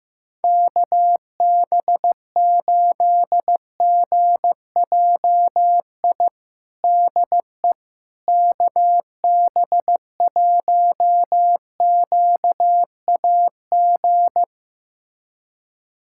Morse code, heard as a keyed tone, about 15 wpm, 700 Hz. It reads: KB8GJI DE KB1QAG
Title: USA Callsign DE Callsign 15wpm